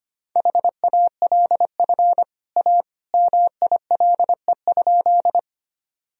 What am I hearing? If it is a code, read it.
Morse code, heard as a keyed tone, about 25 wpm, 700 Hz. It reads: HALF A MILE?
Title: Sentences from Top 1000 Words - Review Part 1 - 25wpm